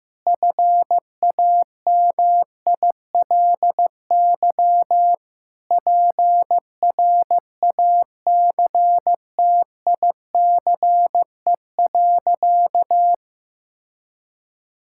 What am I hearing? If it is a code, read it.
Morse code, heard as a keyed tone, about 15 wpm, 700 Hz. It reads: FAMILY PRACTICE.
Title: Sentences from Top 1000 Words - Review Part 5 - 15wpm